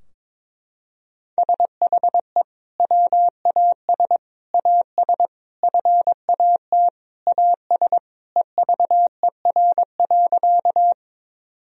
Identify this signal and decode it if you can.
Morse code, heard as a keyed tone, about 22 wpm, 700 Hz. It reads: SHE WAS AS FAT AS EVER.
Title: Sentences from Top 1000 Words - Review Part 4 - 22wpm